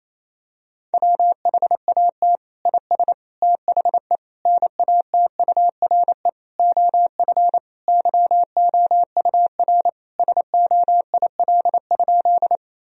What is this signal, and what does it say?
Morse code, heard as a keyed tone, about 28 wpm, 700 Hz. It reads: WHAT IS THE NATURE OF YOUR SOIL?
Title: Sentences from Top 1000 Words - Review Part 4 - 28wpm